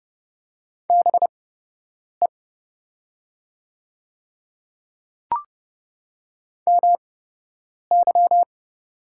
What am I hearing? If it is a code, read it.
Morse code, heard as a keyed tone, about 29 wpm, spaced as for 8 wpm, 700 Hz. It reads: BE MY
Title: 8x Character Spacing - Top 100 Words - 30wpm